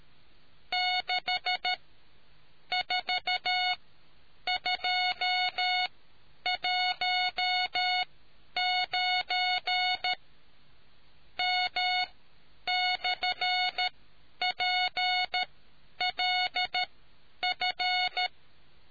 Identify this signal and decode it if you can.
Morse code, heard as a keyed tone, about 13 wpm, spaced as for 8 wpm, 730 Hz. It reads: 64219M/PLF